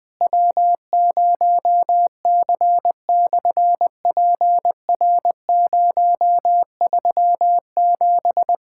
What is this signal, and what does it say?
Morse code, heard as a keyed tone, about 20 wpm, 700 Hz. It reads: W0C/PR037
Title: SOTA QSO Practice 20wpm